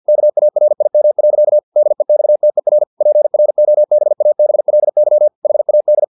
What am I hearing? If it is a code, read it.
Morse code, heard as a keyed tone, about 50 wpm, 600 Hz. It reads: KARIM. DEXTER PROBABLY HAD
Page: Never 49wpm